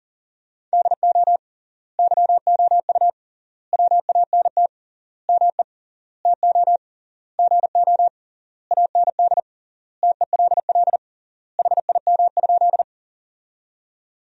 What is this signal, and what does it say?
Morse code, heard as a keyed tone, about 37 wpm, 700 Hz. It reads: DO YOU WANT ME TO GO AND TELL HIM?
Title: 3x Word-Spacing - Sentences from Top 200 Words - 40wpm